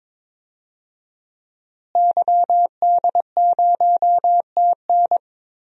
Morse code, heard as a keyed tone, about 22 wpm, 700 Hz. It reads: YD0TN
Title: DXCC-100 Call Signs - 5 Characters - Repeated 3x - 22wpm